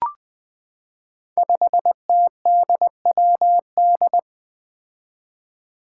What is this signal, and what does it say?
Morse code, heard as a keyed tone, about 20 wpm, 700 Hz. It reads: 5TDWD
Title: Five Challenging Letter-Number Combinations 20wpm